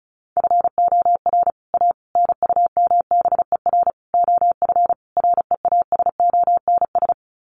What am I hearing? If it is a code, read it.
Morse code, heard as a keyed tone, about 34 wpm, 700 Hz. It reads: FOR A NUMBER OF REASONS
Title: Sets of 5 Words Encore2 35wpm